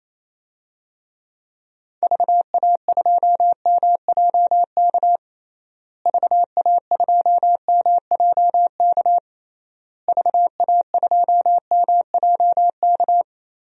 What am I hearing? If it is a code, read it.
Morse code, heard as a keyed tone, about 27 wpm, 700 Hz. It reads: VA2MJK VA2MJK VA2MJK
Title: DXCC-100 Call Signs - 6 Characters - Repeated 3x - 28wpm